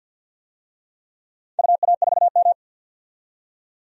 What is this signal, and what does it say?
Morse code, heard as a keyed tone, about 49 wpm, 700 Hz. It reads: UA4M
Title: DXCC-100 Call Signs - 4 Characters - 50wpm